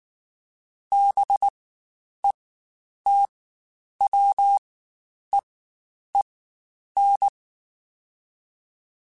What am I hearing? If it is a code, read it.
Morse code, heard as a keyed tone, about 19 wpm, spaced as for 9 wpm, 800 Hz. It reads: BETWEEN
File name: Spaced Characters / Spaced Characters-mp3s / Weather 5 1 sp @18